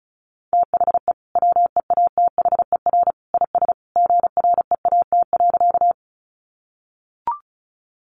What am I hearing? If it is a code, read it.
Morse code, heard as a keyed tone, about 34 wpm, 700 Hz. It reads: THE WEATHER IS GREAT.
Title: Sentences from Top 800 Words 35wpm